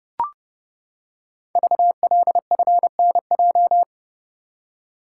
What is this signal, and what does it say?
Morse code, heard as a keyed tone, about 29 wpm, 700 Hz. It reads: VLFNJ
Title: Code Groups - 5 Letters - 30wpm